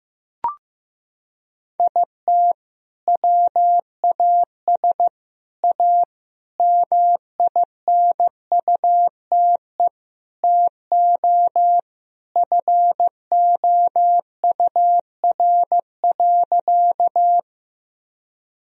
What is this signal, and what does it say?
Morse code, heard as a keyed tone, about 14 wpm, 700 Hz. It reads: IT WAS A MINUTE TO FOUR.
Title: Sentences from Top 1000 Words - Review Part 1 - 15wpm